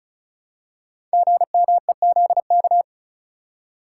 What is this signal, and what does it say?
Morse code, heard as a keyed tone, about 36 wpm, 700 Hz. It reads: GMEZK
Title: Code Groups - 5 Letter:Numbers - 35wpm